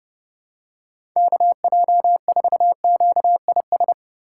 Morse code, heard as a keyed tone, about 30 wpm, 700 Hz. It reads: KJ4QIS
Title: USA Call Signs - 6 Characters - 2x Character Spacing - 30wpm